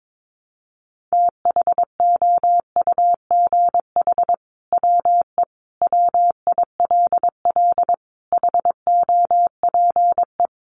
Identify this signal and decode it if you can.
Morse code, heard as a keyed tone, about 22 wpm, 700 Hz. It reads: THOUGH WE WILL HOPE
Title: Sentences from Top 1000 Words 22wpm